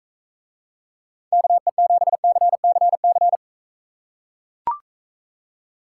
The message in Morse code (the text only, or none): KE7CCC